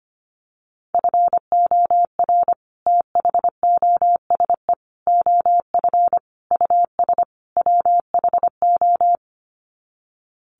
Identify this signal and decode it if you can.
Morse code, heard as a keyed tone, about 25 wpm, 700 Hz. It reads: FOR THOSE OF US WHO
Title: Sets of 5 Words Encore 25wpm